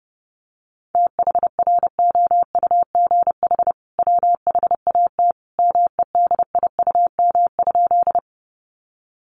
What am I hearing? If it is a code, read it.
Morse code, heard as a keyed tone, about 30 wpm, 700 Hz. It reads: THROUGH WHAT MEDIUM?
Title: Sentences from Top 900 Words 30wpm